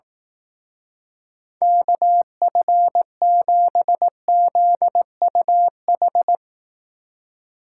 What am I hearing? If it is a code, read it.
Morse code, heard as a keyed tone, about 18 wpm, 700 Hz. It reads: KF7ZUH
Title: Speed Racing - USA 2x3 Call Signs 15wpm